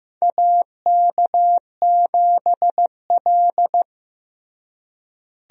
Morse code, heard as a keyed tone, about 15 wpm, 700 Hz. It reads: AK7L
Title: USA Call Signs - 4 Characters - Repeated 3x - 15wpm